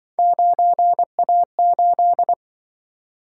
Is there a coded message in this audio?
Yes: Morse code from 0.2 to 2.3 s, heard as a keyed tone, about 26 wpm, 700 Hz. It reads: 9A8